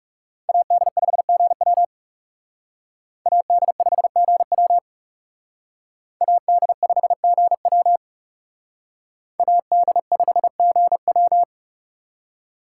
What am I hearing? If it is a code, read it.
Morse code, heard as a keyed tone, about 38 wpm, 700 Hz. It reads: AD5GW AD5GW AD5GW AD5GW